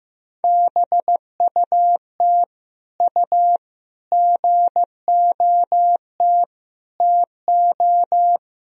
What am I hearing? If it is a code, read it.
Morse code, heard as a keyed tone, about 15 wpm, 700 Hz. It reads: BUT U GOT TO
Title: Sentences from Top 300 Words and Abbreviations 15wpm